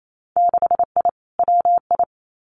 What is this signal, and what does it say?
Morse code, heard as a keyed tone, about 28 wpm, 700 Hz. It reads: 6I WI
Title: Winter Field Day Exchanges Encore 28wpm